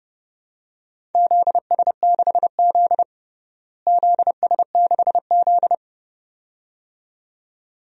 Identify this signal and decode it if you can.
Morse code, heard as a keyed tone, about 30 wpm, 700 Hz. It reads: ZS6Z ZS6Z
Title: DXCC-100 Call Signs - 4 Characters - Repeated 2x - 30wpm